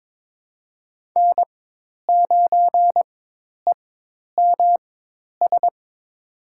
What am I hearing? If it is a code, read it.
Morse code, heard as a keyed tone, about 21 wpm, spaced as for 10 wpm, 700 Hz. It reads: N9EMS
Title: USA Call Signs - 5 Characters - 4x Character Spacing - 22wpm